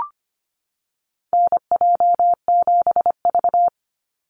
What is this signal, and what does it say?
Morse code, heard as a keyed tone, about 25 wpm, 700 Hz. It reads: NJ7V
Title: NA SOTA Activator Callsigns 25wpm